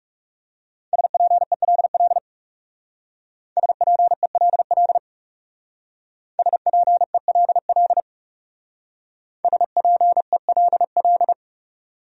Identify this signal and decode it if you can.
Morse code, heard as a keyed tone, about 38 wpm, 700 Hz. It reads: SPELL SPELL SPELL SPELL